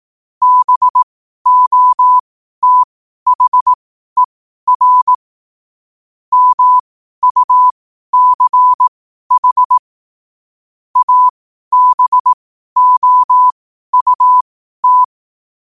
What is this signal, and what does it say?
Morse code, heard as a keyed tone, about 18 wpm, spaced as for 12 wpm, 1000 Hz. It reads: BOTHER MUCH ABOUT
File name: qotdfiles / 12wpm